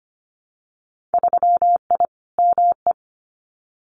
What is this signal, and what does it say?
Morse code, heard as a keyed tone, about 25 wpm, 690 Hz. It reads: 3I ME
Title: Winter Field Day Exchanges 25wpm